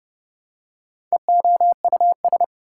Morse code, heard as a keyed tone, about 29 wpm, 700 Hz. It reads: EOUS
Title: Prefixes and Suffixes 30wpm